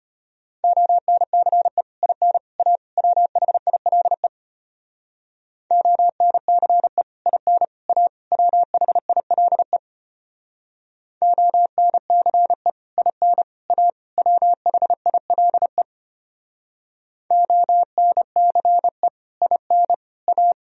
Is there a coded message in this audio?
Yes: Morse code from 0.6 to 20.5 s, heard as a keyed tone, about 33 wpm, 700 Hz. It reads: ONCE IN A WHILE ONCE IN A WHILE ONCE IN A WHILE ONCE IN A